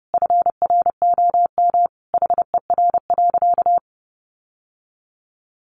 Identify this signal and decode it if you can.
Morse code, heard as a keyed tone, about 30 wpm, 700 Hz. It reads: FROM HER.
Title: Sentences from Top 1300 Words 30wpm